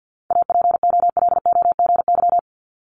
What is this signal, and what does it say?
Morse code, heard as a keyed tone, about 50 wpm, 690 Hz. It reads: APOLOGY
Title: Multi-letter Phonemes Repeated 3x 50wpm